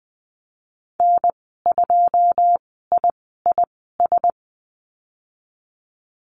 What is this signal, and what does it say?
Morse code, heard as a keyed tone, about 20 wpm, spaced as for 15 wpm, 700 Hz. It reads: N2IIS
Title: 088 Lesson 15g - Number 5 - Call Signs 20wpm